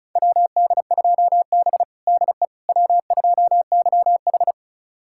Morse code, heard as a keyed tone, about 36 wpm, 700 Hz. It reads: WD2B DE W2YH
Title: USA Callsign DE Callsign - 4 Characters - 35wpm